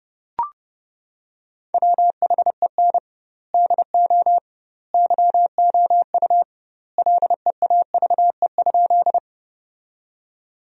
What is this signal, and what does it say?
Morse code, heard as a keyed tone, about 32 wpm, 700 Hz. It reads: WHEN DO YOU LEAVE?
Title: 2x Word-Spacing - Sentences from Top 400 Words - 30wpm